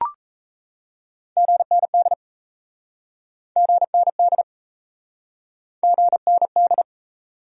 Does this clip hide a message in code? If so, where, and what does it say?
Yes: Morse code from 1.4 to 6.8 s, heard as a keyed tone, about 37 wpm, 700 Hz. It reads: GND GND GND